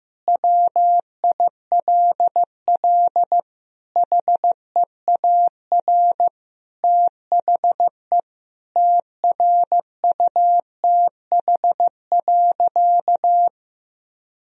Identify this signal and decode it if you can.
Morse code, heard as a keyed tone, about 15 wpm, 700 Hz. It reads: WILL HEAR THE TRUTH.
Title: Sentences from Top 1000 Words - Review Part 1 - 15wpm